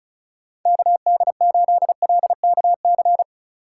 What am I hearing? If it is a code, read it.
Morse code, heard as a keyed tone, about 34 wpm, 700 Hz. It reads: KD8LKC
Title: USA Call Signs - 6 Characters - 4x Character Spacing - 35wpm